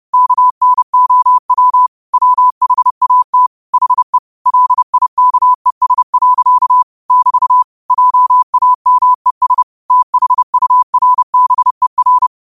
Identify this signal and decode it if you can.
Morse code, heard as a keyed tone, about 30 wpm, 1000 Hz. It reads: KNOW WHAT HE LIKES. <BT> JAMES THURBER
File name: qotdfiles / 30wpm